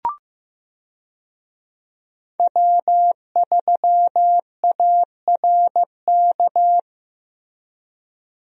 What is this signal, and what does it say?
Morse code, heard as a keyed tone, about 15 wpm, 700 Hz. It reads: W3ARK
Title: USA Call Signs - 5 Characters - Repeated 2x - 15wpm